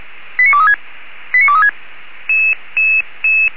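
Five-tone selective call (ZVEI), heard as quick runs of five tones, tones of about 70 ms. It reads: 89236 89236
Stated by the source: ZVEI five-tone selective call 89240, received by SDR.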